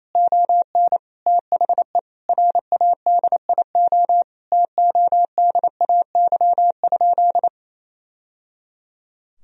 Morse code, heard as a keyed tone, about 28 wpm, 700 Hz. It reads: ON THE RADIO TODAY?
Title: Sentences from Top 1000 Words - Review Part 3 - 28wpm